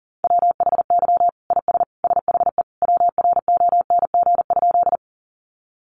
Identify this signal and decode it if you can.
Morse code, heard as a keyed tone, about 40 wpm, 700 Hz. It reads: WHY IS SHE WRONG?
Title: Sentences from Top 900 Words 40wpm